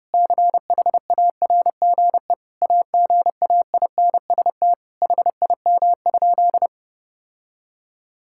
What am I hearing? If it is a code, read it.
Morse code, heard as a keyed tone, about 32 wpm, 700 Hz. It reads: CHARGE AGAINST HIM?
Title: Sentences from Top 1000 Words - Review Part 6 - 30wpm